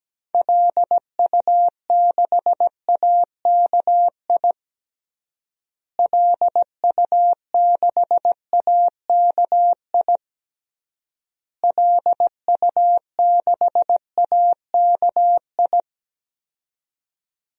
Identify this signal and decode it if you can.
Morse code, heard as a keyed tone, about 17 wpm, 700 Hz. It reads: LU6AKI LU6AKI LU6AKI
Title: DXCC-100 Call Signs - 6 Characters - Repeated 3x - 17wpm